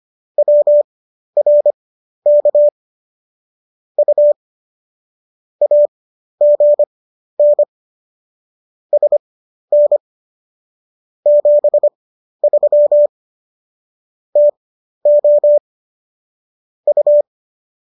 Morse code, heard as a keyed tone, about 25 wpm, spaced as for 12 wpm, 590 Hz. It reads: WRK U AGN SN 73 TO U
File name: qso009 12